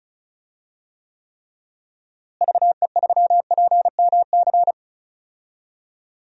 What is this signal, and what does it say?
Morse code, heard as a keyed tone, about 36 wpm, 700 Hz. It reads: VE3PMC
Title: DXCC-100 Call Signs - 6 Characters - Repeated 3x - 35wpm